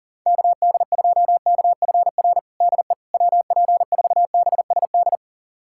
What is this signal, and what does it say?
Morse code, heard as a keyed tone, about 37 wpm, 700 Hz. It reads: KD2KFR DE WP4BSD
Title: USA Callsign DE Callsign 40wpm